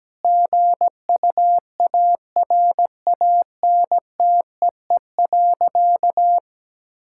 Morse code, heard as a keyed tone, about 17 wpm, 700 Hz. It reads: GUARANTEE.
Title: Sentences from Top 1600 Words 17wpm